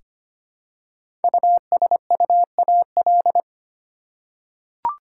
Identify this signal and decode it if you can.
Morse code, heard as a keyed tone, about 25 wpm, 700 Hz. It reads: USUAL
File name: 5 Letter Words - First 3 Letters Given - 25wpm